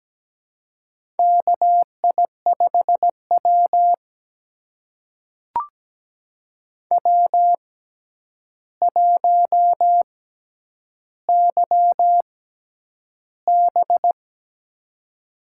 Morse code, heard as a keyed tone, about 17 wpm, spaced as for 9 wpm, 700 Hz. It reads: KI5W W1YB
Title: USA Call Signs - 6x Character Spacing - 17wpm